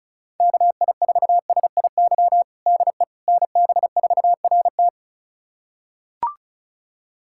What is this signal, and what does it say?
Morse code, heard as a keyed tone, about 34 wpm, 700 Hz. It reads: KI4SIY DE NB4RT
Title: USA Callsign DE Callsign 35wpm